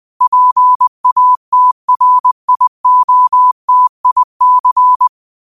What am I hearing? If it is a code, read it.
Morse code, heard as a keyed tone, about 20 wpm, 1000 Hz. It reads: PATRIOTIC